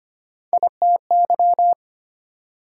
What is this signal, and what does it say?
Morse code, heard as a keyed tone, about 34 wpm, spaced as for 30 wpm, 700 Hz. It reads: ITY
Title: Speed-Racing Multi-letter Phonemes - 25wpm